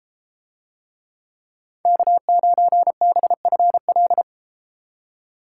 Morse code, heard as a keyed tone, about 32 wpm, 700 Hz. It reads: K9BFL